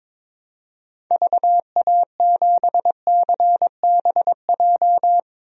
Morse code, heard as a keyed tone, about 22 wpm, 700 Hz. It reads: VA7CBJ
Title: DXCC-100 Call Signs - 6 Characters - Repeated 3x - 22wpm